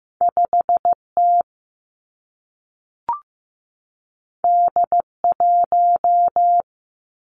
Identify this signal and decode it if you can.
Morse code, heard as a keyed tone, about 15 wpm, 700 Hz. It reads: 5 T D 1
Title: Two Challenging Letter-Number Combinations 15wpm